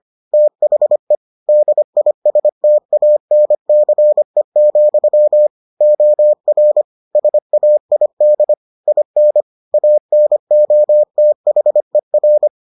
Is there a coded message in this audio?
Yes: Morse code from 0.3 to 12.5 s, heard as a keyed tone, about 25 wpm, 600 Hz. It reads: THE DISTANCE, OR SAID IN ANOTHER